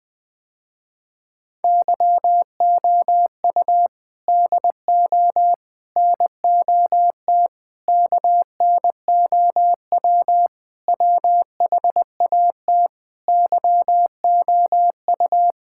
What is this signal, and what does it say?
Morse code, heard as a keyed tone, about 21 wpm, 700 Hz. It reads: YOU DO NOT KNOW WHAT YOU
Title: Sentences with 5-Word Phrases - Spoken Beginning - 20wpm